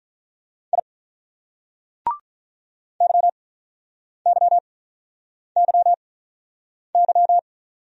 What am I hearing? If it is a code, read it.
Morse code, heard as a keyed tone, about 48 wpm, spaced as for 9 wpm, 700 Hz. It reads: I YYYY